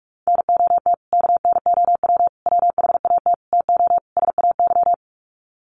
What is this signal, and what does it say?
Morse code, heard as a keyed tone, about 44 wpm, 700 Hz. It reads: NOT KNOW WHAT TO SAY
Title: Sets of 5 Words 45wpm